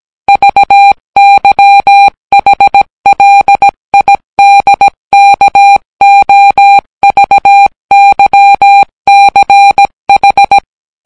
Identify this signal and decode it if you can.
Morse code, heard as a keyed tone, about 17 wpm, 800 Hz. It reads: VYHLIDKOVYCH